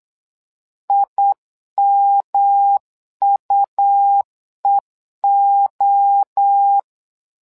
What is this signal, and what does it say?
Morse code, heard as a keyed tone, about 8 wpm, 800 Hz. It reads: IMUEO
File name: LEZIONI / 3 MWRO / 3 MWRO 8wpm 2606